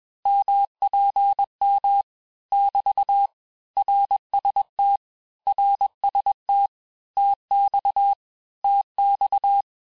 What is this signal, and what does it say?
Morse code, heard as a keyed tone, about 21 wpm, 800 Hz. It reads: MPM <BT> RST RST TX TX
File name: cours-du-lundi files / Cours-1586-texte-abreviations-vitesse-15mpm-puis-20mpm